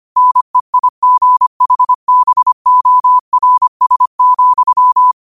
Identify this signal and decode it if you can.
Morse code, heard as a keyed tone, about 25 wpm, 1000 Hz. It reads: NEIGHBORS,
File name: qotdfiles / 25wpm